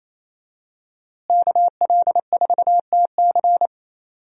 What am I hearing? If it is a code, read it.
Morse code, heard as a keyed tone, about 28 wpm, 700 Hz. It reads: KL4TC
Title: USA Call Signs - 5 Characters - Repeated 3x - 28wpm